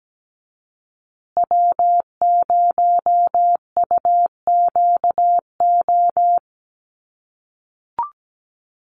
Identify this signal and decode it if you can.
Morse code, heard as a keyed tone, about 17 wpm, 700 Hz. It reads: W0UQO
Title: USA Call Signs Repeated 2x 17wpm